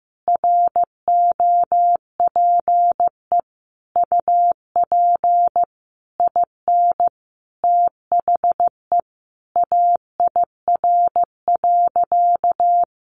Morse code, heard as a keyed tone, about 15 wpm, 700 Hz. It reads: ROPE UP IN THE AIR.
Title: Easy Phrases 15wpm